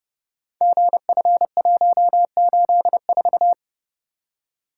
Morse code, heard as a keyed tone, about 30 wpm, 700 Hz. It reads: GF184
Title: Code Groups - 5 Letter:Numbers - 30wpm